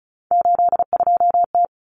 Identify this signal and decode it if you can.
Morse code, heard as a keyed tone, about 34 wpm, 700 Hz. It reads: 82T
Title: Common Cut Numbers - 3-Digit 35wpm